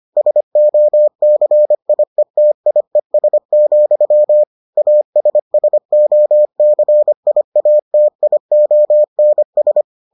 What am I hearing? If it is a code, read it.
Morse code, heard as a keyed tone, about 25 wpm, 600 Hz. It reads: SOCIETIES, ASSOCIATIONS